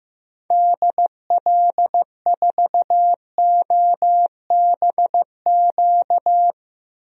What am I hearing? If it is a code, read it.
Morse code, heard as a keyed tone, about 15 wpm, 700 Hz. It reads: DL4OBQ
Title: USA Call Signs - 6 Characters - Repeated 3x - 15wpm